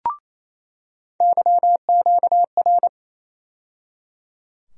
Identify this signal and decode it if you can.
Morse code, heard as a keyed tone, about 28 wpm, 700 Hz. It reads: YQR
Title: USA Call Sign Postfixes - 3 Letters 28wpm